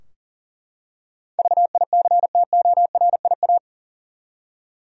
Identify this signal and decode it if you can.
Morse code, heard as a keyed tone, about 39 wpm, 700 Hz. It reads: VICTORIA
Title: USA Common Names - 40wpm